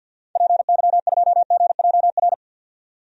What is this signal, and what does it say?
Morse code, heard as a keyed tone, about 52 wpm, 700 Hz. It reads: PY2GJR
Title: DXCC-100 Call Signs - 6 Characters - 50wpm